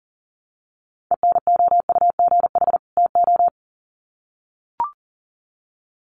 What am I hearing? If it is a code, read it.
Morse code, heard as a keyed tone, about 37 wpm, 700 Hz. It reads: ENOUGH TO
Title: Sets of 2 Words 40wpm